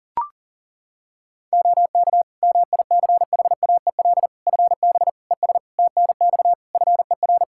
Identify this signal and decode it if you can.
Morse code, heard as a keyed tone, about 40 wpm, 700 Hz. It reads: OK MICHAEL FB ES TNX FER
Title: K1USN SST QSO Practice 40wpm